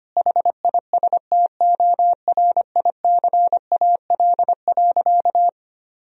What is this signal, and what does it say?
Morse code, heard as a keyed tone, about 25 wpm, 700 Hz. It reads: HISTORICAL.
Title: Sentences from Top 1000 Words - Review Part 4 - 25wpm